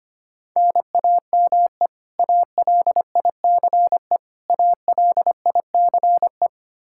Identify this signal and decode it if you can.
Morse code, heard as a keyed tone, about 25 wpm, 700 Hz. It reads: NAME ALICE ALICE
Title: Practice QSO - Exchange 1 - Beginning - 25wpm